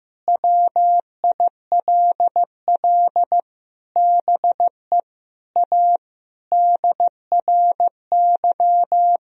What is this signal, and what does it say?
Morse code, heard as a keyed tone, about 16 wpm, 700 Hz. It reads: WILL BE A DRY
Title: Sentences from Top 1000 Words - Review Part 3 - 15wpm